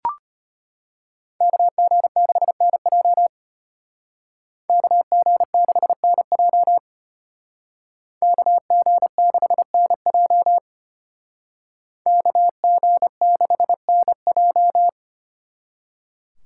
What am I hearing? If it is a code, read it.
Morse code, heard as a keyed tone, about 33 wpm, 700 Hz. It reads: KG6NJ KG6NJ KG6NJ KG6NJ